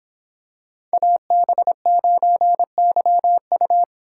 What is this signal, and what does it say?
Morse code, heard as a keyed tone, about 26 wpm, 700 Hz. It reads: AB9YU